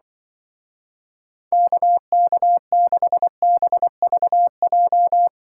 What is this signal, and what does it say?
Morse code, heard as a keyed tone, about 24 wpm, 700 Hz. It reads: KK6BVJ
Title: Speed Racing - USA 2x3 Call Signs 20wpm